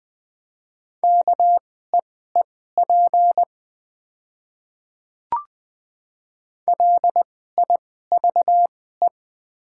Morse code, heard as a keyed tone, about 20 wpm, spaced as for 14 wpm, 700 Hz. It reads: KEEP LIVE
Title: 2x Character Spacing - Top 300 Words - 20wpm